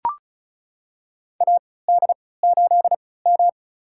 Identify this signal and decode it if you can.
Morse code, heard as a keyed tone, about 35 wpm, spaced as for 20 wpm, 700 Hz. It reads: AD8M
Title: USA Call Signs - 4 Characters - 3x Character Spacing - 35wpm